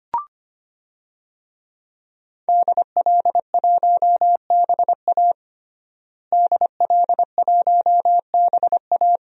Morse code, heard as a keyed tone, about 25 wpm, 700 Hz. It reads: DL1BA DL1BA
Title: DXCC-100 Call Signs - 5 Characters - Repeated 2x - 25wpm